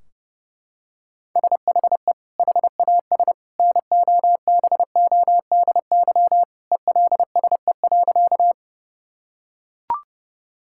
Morse code, heard as a keyed tone, about 30 wpm, 700 Hz. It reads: SHE HAS NOBODY ELSE.
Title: Sentences from Top 1600 Words 30wpm